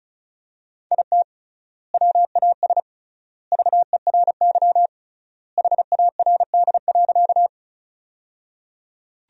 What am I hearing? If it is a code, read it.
Morse code, heard as a keyed tone, about 36 wpm, 700 Hz. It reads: IT WAS VERY HARD.
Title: 3x Word-Spacing - Sentences from Top 300 Words - 35wpm